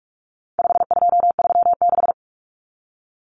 Morse code, heard as a keyed tone, about 44 wpm, 700 Hz. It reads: 5236
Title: Numbers - 4-Digit 45wpm